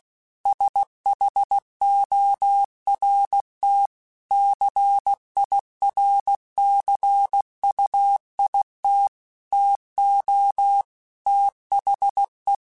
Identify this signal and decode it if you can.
Morse code, heard as a keyed tone, about 16 wpm, 800 Hz. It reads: SHORT CIRCUIT TO THE
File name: Technical Subjects / Technical Subjects-mp3s / XC @15